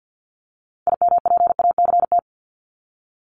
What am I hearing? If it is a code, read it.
Morse code, heard as a keyed tone, about 50 wpm, 700 Hz. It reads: IMPACT
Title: Top 751-1000 Words 50wpm